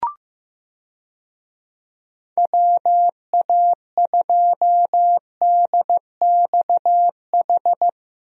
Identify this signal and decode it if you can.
Morse code, heard as a keyed tone, about 15 wpm, 700 Hz. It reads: WA2DXH
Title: USA Call Signs - 6 Characters - Repeated 2x - 15wpm